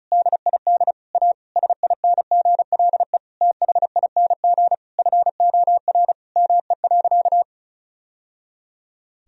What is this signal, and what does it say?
Morse code, heard as a keyed tone, about 36 wpm, 700 Hz. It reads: DID A SINGLE THING FOR ME.
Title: Sentences from Top 1000 Words - Review Part 4 - 35wpm